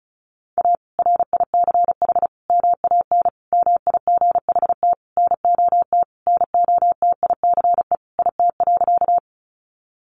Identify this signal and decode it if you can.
Morse code, heard as a keyed tone, about 34 wpm, 700 Hz. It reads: A RICH MAN MIGHT NOT NOTICE IT.
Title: Sentences from Top 1100 Words 35wpm